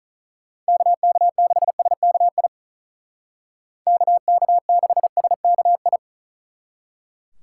Morse code, heard as a keyed tone, about 41 wpm, 700 Hz. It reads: KK6SKI KK6SKI
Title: Speed Racing - USA Call Signs 35wpm